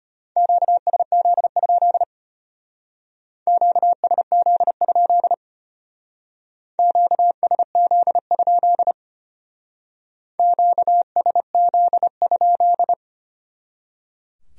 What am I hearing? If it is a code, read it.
Morse code, heard as a keyed tone, about 33 wpm, 700 Hz. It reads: QSZ? QSZ? QSZ? QSZ?